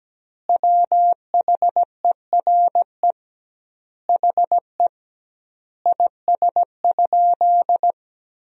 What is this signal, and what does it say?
Morse code, heard as a keyed tone, about 17 wpm, 700 Hz. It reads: WHERE HE IS?
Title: 2x Word-Spacing - Sentences from Top 300 Words - 17wpm